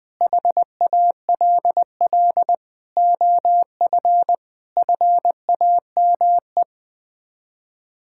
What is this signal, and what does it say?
Morse code, heard as a keyed tone, about 20 wpm, 700 Hz. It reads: HALL OF FAME
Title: Sets of 6 Words - 20wpm